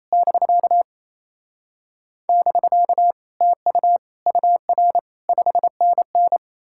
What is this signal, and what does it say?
Morse code, heard as a keyed tone, about 38 wpm, spaced as for 34 wpm, 700 Hz. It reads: <BK> <BK> TU UR 5NN
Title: Speed-Racing Common QSO Phrases - 28wpm